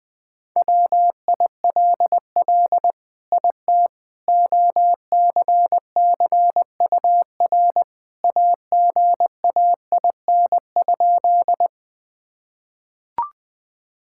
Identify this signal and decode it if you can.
Morse code, heard as a keyed tone, about 20 wpm, 700 Hz. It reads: WILL IT OCCUR AGAIN?
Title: Sentences from Top 1000 Words - Review Part 6 - 20wpm